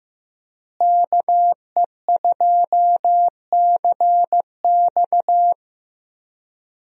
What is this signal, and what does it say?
Morse code, heard as a keyed tone, about 15 wpm, 700 Hz. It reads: KE2CX
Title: USA Call Signs - 5 Characters - Repeated 2x - 15wpm